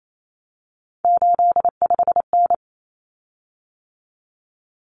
Morse code, heard as a keyed tone, about 28 wpm, 700 Hz. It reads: 85N
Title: Common Cut Numbers - 3-Digit 28wpm